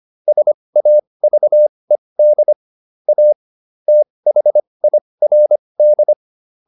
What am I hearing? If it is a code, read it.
Morse code, heard as a keyed tone, about 25 wpm, spaced as for 20 wpm, 600 Hz. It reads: SAVED A THIRD